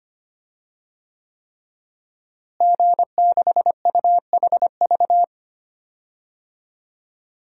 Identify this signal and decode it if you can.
Morse code, heard as a keyed tone, about 25 wpm, 700 Hz. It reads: G6UHV
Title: DXCC-100 Call Signs - 5 Characters - Repeated 3x - 25wpm